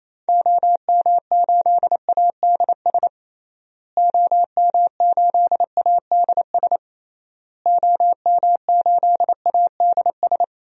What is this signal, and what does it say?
Morse code, heard as a keyed tone, about 28 wpm, 700 Hz. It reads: OM8ADS OM8ADS OM8ADS
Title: DXCC-100 Call Signs - 6 Characters - Repeated 3x - 28wpm